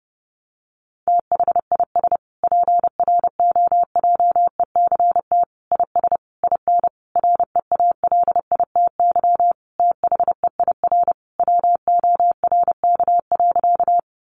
Morse code, heard as a keyed tone, about 28 wpm, 700 Hz. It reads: THIS PROJECT IS IN REALITY THEIR WORK.
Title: Sentences from Top 1400 Words 30wpm